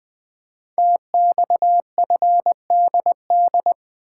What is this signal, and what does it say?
Morse code, heard as a keyed tone, about 20 wpm, 700 Hz. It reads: TXFDD
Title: Code Groups - 5 Letters - 20wpm